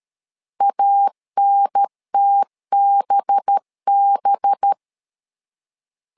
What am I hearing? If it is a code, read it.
Morse code, heard as a keyed tone, about 13 wpm, 800 Hz. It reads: ANTBB
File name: LEZIONI / 5 LBVC / 5 LB 12wpm 1906